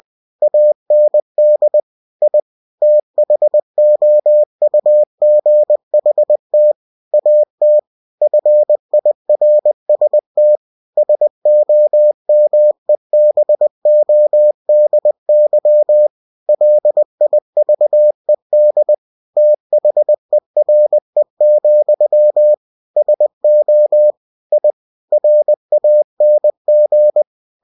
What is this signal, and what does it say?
Morse code, heard as a keyed tone, about 20 wpm, 600 Hz. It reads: AND I THOUGHT AT FIRST SOMEBODY LIVED THERE, SO I RANG
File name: cw-books / winnie-the-pooh / winnie-20wpm-page-0032